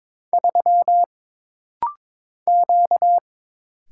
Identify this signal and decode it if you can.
Morse code, heard as a keyed tone, about 22 wpm, 700 Hz. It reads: 3 Q
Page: This practice set is composed of Letters, Numbers, common punctuations, and common prosigns.